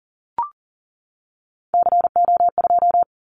Morse code, heard as a keyed tone, about 38 wpm, 700 Hz. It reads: CO2
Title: Callsign Prefixes for Top 100 DXCC - 40wpm